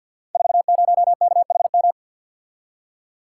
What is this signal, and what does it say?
Morse code, heard as a keyed tone, about 52 wpm, 700 Hz. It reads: 40KHM
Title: Code Groups - 5 Letter:Numbers - 50wpm